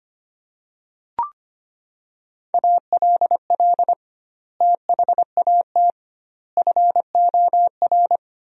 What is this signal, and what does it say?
Morse code, heard as a keyed tone, about 25 wpm, 700 Hz. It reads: ALL THAT FOR
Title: 2x Word-Spacing - Sentences from Top 500 Words - 25wpm